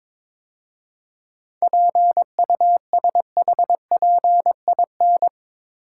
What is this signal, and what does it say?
Morse code, heard as a keyed tone, about 24 wpm, 690 Hz. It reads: PUSHPIN
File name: Easy Words - Speed-Racing with Hints - 22wpm